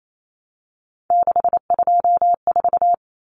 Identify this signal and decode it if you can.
Morse code, heard as a keyed tone, about 28 wpm, 700 Hz. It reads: 624
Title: Numbers - 3-Digit 28wpm